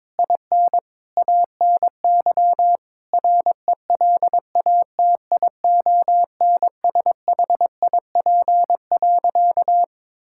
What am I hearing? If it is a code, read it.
Morse code, heard as a keyed tone, about 22 wpm, 700 Hz. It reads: IN ANY RELATIONSHIP.
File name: Difficult Sentences with Hints - 22wpm